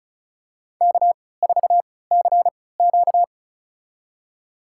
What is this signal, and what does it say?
Morse code, heard as a keyed tone, about 36 wpm, spaced as for 20 wpm, 700 Hz. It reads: K4CQ
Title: USA Call Signs - 4 Characters - 3x Character Spacing - 35wpm